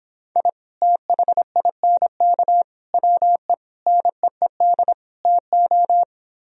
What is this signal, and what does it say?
Morse code, heard as a keyed tone, about 26 wpm, 700 Hz. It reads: I THINK WE NEED TO